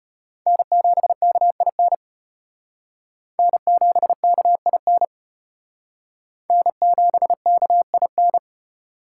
Morse code, heard as a keyed tone, about 35 wpm, 700 Hz. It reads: N7KIN N7KIN N7KIN